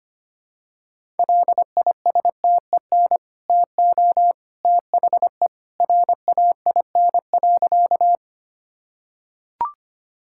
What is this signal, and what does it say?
Morse code, heard as a keyed tone, about 25 wpm, 700 Hz. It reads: LISTEN TO THE RAIN.
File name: Easy Sentences with Hints - 25wpm